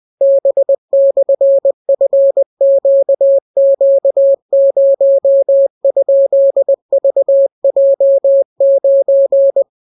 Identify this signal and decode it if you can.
Morse code, heard as a keyed tone, about 20 wpm, 550 Hz. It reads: B/FQQ0?VJ9